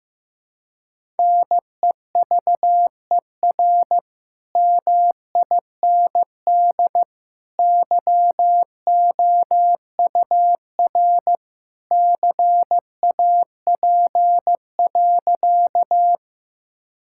Morse code, heard as a keyed tone, about 15 wpm, 700 Hz. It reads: NEVER MIND YOUR CAP.
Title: Sentences from Top 1600 Words 15wpm